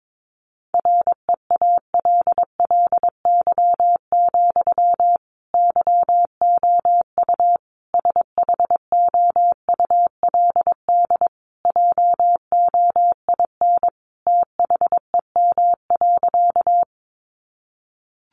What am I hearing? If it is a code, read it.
Morse code, heard as a keyed tone, about 22 wpm, 700 Hz. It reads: REALLY, YOU SHOULD JOIN THEM.
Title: Sentences from Top 1400 Words 22wpm